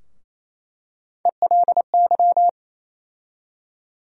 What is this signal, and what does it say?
Morse code, heard as a keyed tone, about 28 wpm, 700 Hz. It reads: ELY
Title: USA Call Sign Postfixes - 3 Letters 28wpm